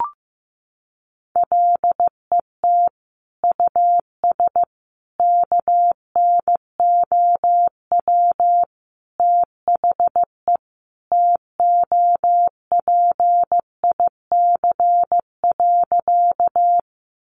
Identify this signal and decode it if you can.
Morse code, heard as a keyed tone, about 15 wpm, 700 Hz. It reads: LET US KNOW THE TOPIC.
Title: Sentences from Top 1200 Words 15wpm